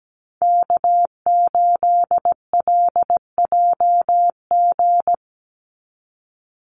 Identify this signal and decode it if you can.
Morse code, heard as a keyed tone, about 17 wpm, 700 Hz. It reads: K8LJG
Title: NA SOTA Chaser Callsigns 17wpm